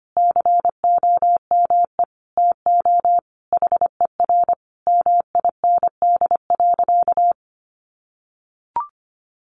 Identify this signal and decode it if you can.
Morse code, heard as a keyed tone, about 25 wpm, 700 Hz. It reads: COME TO HER MIND.
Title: Sentences from Top 1300 Words 25wpm